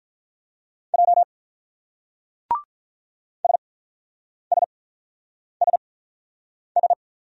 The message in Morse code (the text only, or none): J SSSS